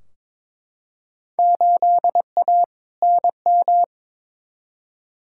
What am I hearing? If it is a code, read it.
Morse code, heard as a keyed tone, about 22 wpm, 700 Hz. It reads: 8A NM
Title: Field Day Exchanges Encore - 22wpm